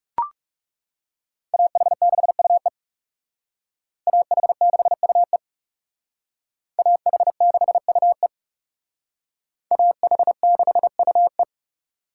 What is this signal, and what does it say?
Morse code, heard as a keyed tone, about 39 wpm, 700 Hz. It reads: AH6UE AH6UE AH6UE AH6UE